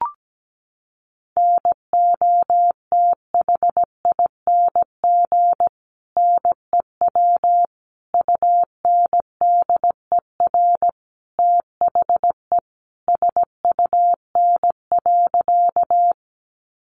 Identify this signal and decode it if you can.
Morse code, heard as a keyed tone, about 17 wpm, 700 Hz. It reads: NOTHING NEW UNDER THE SUN.
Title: Sentences from Top 500 Words 17wpm